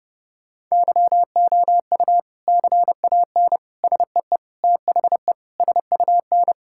Morse code, heard as a keyed tone, about 30 wpm, 700 Hz. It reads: YOU CAN SEE THE SUN
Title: Sentences from Top 1600 Words 30wpm